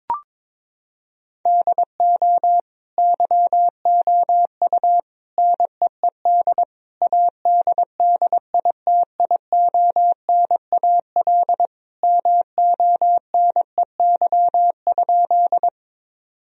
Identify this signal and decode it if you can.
Morse code, heard as a keyed tone, about 22 wpm, 700 Hz. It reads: DO YOU NEED ADDITIONAL MONEY?
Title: Sentences from Top 1000 Words - Review Part 4 - 22wpm